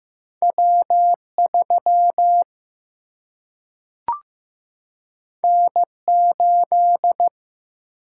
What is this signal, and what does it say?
Morse code, heard as a keyed tone, about 15 wpm, 700 Hz. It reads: W 3 N 8
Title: Single-Letter Single-Number 15wpm